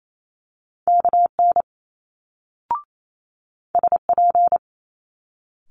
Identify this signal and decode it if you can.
Morse code, heard as a keyed tone, about 28 wpm, 700 Hz. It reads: KN SP
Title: Multi-letter Phonemes Quick Fire 28wpm